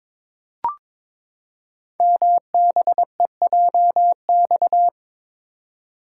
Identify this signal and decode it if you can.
Morse code, heard as a keyed tone, about 22 wpm, 700 Hz. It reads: MBEJX
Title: Code Groups - 5 Letters - 22wpm